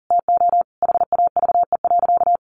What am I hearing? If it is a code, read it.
Morse code, heard as a keyed tone, about 40 wpm, 700 Hz. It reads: TO HAVE.
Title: Sentences from Top 1200 Words 40wpm